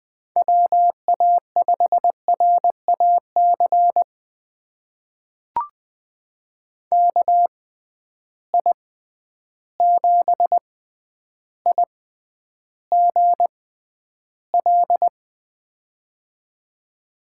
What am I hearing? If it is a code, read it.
Morse code, heard as a keyed tone, about 20 wpm, spaced as for 10 wpm, 700 Hz. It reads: WA5RAC KI7IGL
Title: USA Call Signs - 6 Characters - 6x Character Spacing - 20wpm